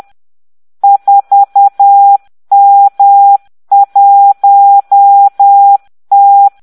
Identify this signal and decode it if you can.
Morse code, heard as a keyed tone, about 10 wpm, 800 Hz. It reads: 4M1T